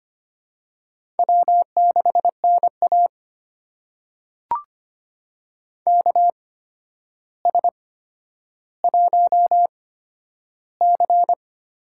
Morse code, heard as a keyed tone, about 25 wpm, spaced as for 11 wpm, 700 Hz. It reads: W6NA KS1C
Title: USA Call Signs - 4 Characters - 8x Character Spacing - 25wpm